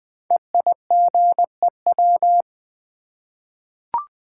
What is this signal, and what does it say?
Morse code, heard as a keyed tone, about 20 wpm, 700 Hz. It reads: EIGEW
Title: Code Groups - 5 Letters - 20wpm